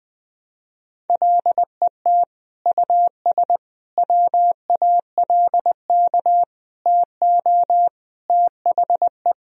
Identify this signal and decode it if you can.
Morse code, heard as a keyed tone, about 20 wpm, 700 Hz. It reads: LET US WALK TO THE
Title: Sentences from Top 1000 Words - Review Part 2 - 20wpm